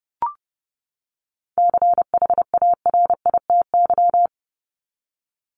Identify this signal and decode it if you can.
Morse code, heard as a keyed tone, about 30 wpm, 700 Hz. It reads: CHARITY
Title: Top 1501-1750 Words 30wpm